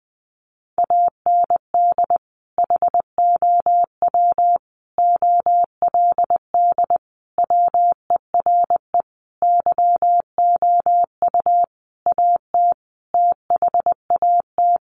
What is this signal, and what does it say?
Morse code, heard as a keyed tone, about 20 wpm, 700 Hz. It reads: AND HOW OLD WERE YOU AT THAT
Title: Sentences from Top 200 Words 20wpm